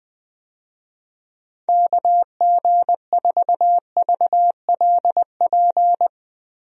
Morse code, heard as a keyed tone, about 20 wpm, 700 Hz. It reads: KG4VLP
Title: USA Call Signs - 6 Characters - 20wpm